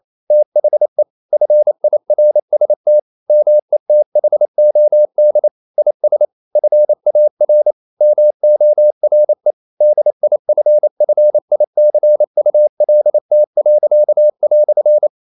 Morse code, heard as a keyed tone, about 28 wpm, 600 Hz. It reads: THE FIRST METHOD IS FAR MORE DIFFICULT."
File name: CW / Practice / EnglishQuotations0228